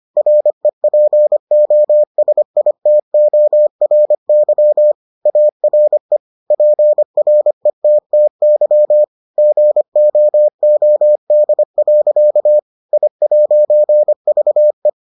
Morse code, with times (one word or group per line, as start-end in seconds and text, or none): 0.2-4.9 REPOSITORY
5.3-6.2 ARE
6.5-9.0 PRETTY
9.4-12.6 GOOD.
12.9-14.9 I'VE